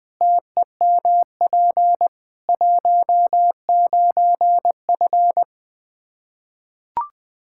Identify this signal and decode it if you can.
Morse code, heard as a keyed tone, about 21 wpm, 700 Hz. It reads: TEMP 19F
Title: Practice QSO - Exchange 2 - Info - 20wpm